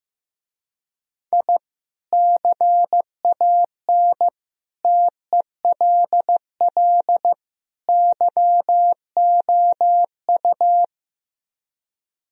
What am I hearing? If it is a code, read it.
Morse code, heard as a keyed tone, about 15 wpm, 700 Hz. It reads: I CAN TELL YOU
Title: Speed-Racing - Sets of 4 Words 15wpm